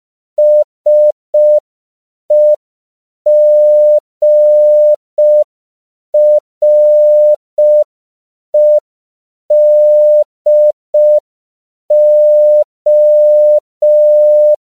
Morse code, heard as a keyed tone, about 5 wpm, 600 Hz. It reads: SEGREDO
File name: 5PPM